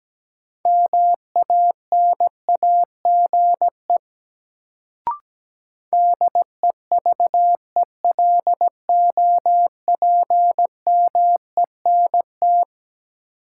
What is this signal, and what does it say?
Morse code, heard as a keyed tone, about 17 wpm, 700 Hz. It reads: MANAGE DEVELOPMENT
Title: Sentences from Top 1400 Words - New Word Drill Alt - 17wpm